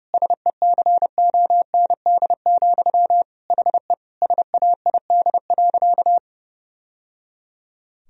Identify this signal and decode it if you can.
Morse code, heard as a keyed tone, about 30 wpm, 700 Hz. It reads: SECOND, HE SAID.
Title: Sentences from Top 1000 Words - Review Part 1 - 30wpm